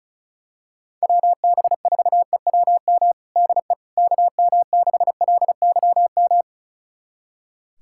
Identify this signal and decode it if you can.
Morse code, heard as a keyed tone, about 36 wpm, 700 Hz. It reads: WB4EWM DE KM6LYM
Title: USA Callsign DE Callsign 35wpm